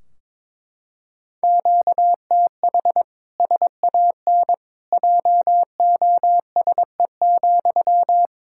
Morse code, heard as a keyed tone, about 22 wpm, 700 Hz. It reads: QTH SAN JOSE,
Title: Practice QSO - Exchange 1 - Beginning - 22wpm